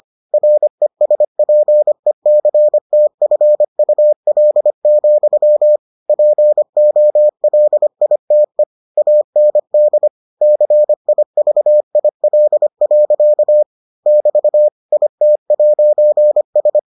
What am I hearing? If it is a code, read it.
Morse code, heard as a keyed tone, about 25 wpm, 600 Hz. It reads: RESPECTFUL, POLITE AND CIVIL. <BT> IT'S